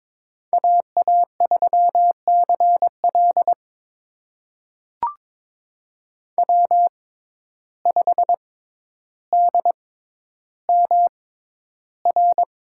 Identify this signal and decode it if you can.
Morse code, heard as a keyed tone, about 22 wpm, spaced as for 11 wpm, 700 Hz. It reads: AA3CL W5DMR
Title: USA Call Signs - 5 Characters - 6x Character Spacing - 22wpm